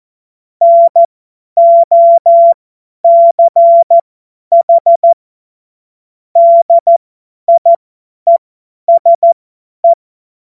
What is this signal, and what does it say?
Morse code, heard as a keyed tone, about 14 wpm, spaced as for 10 wpm, 680 Hz. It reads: NOCH DIESE
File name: Morsekurs / Witze / Witz 19. Oktober 2021, 50 F70